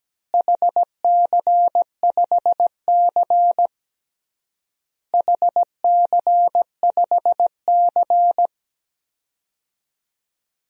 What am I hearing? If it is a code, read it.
Morse code, heard as a keyed tone, about 17 wpm, 700 Hz. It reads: HC5C HC5C
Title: DXCC-100 Call Signs - 4 Characters - Repeated 3x - 17wpm